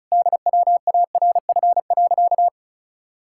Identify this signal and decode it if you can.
Morse code, heard as a keyed tone, about 36 wpm, 700 Hz. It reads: DWARF.
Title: Pangram Challenge v2 - 35wpm